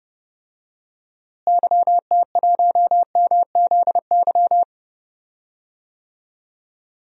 Morse code, heard as a keyed tone, about 32 wpm, 700 Hz. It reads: YT1MZY
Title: DXCC-100 Call Signs - 6 Characters - Repeated 2x - 30wpm